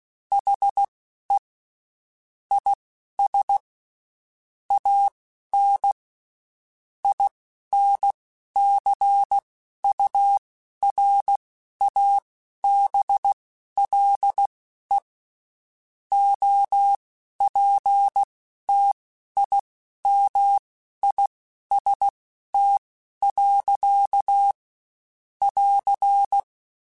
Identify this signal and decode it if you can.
Morse code, heard as a keyed tone, about 16 wpm, spaced as for 11 wpm, 800 Hz. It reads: HE IS AN INCURABLE OPTIMIST. <AR>
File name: Quotes Spaced / Quotes Spaced-mp3s / Optimist sp @15